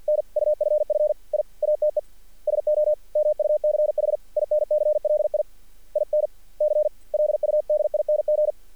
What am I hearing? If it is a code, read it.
Morse code, heard as a keyed tone, about 46 wpm, 600 Hz. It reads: NUFF I ATE SO MUCH INCLI INCLUDING
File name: archived / CRSnet